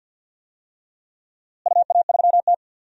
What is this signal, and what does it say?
Morse code, heard as a keyed tone, about 47 wpm, 700 Hz. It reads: UA3T
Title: DXCC-100 Call Signs - 4 Characters - Repeated 3x - 50wpm